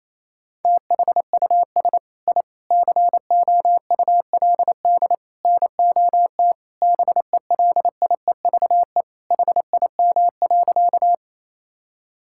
Morse code, heard as a keyed tone, about 28 wpm, 700 Hz. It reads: THUS I COULD NOT BELIEVE HIM.
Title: Sentences from Top 1000 Words - Review Part 5 - 28wpm